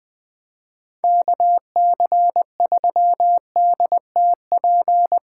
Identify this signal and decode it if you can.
Morse code, heard as a keyed tone, about 24 wpm, 700 Hz. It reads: KC3DTP
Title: Speed Racing - USA Call Signs 20wpm